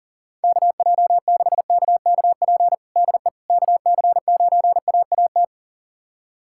Morse code, heard as a keyed tone, about 40 wpm, 700 Hz. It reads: KJ6KKP DE KC9AAT
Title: USA Callsign DE Callsign - 6 Characters - 40wpm